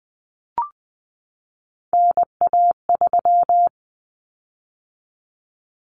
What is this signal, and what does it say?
Morse code, heard as a keyed tone, about 20 wpm, 700 Hz. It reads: NA3
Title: Common Cut Numbers - 3-Digit 20wpm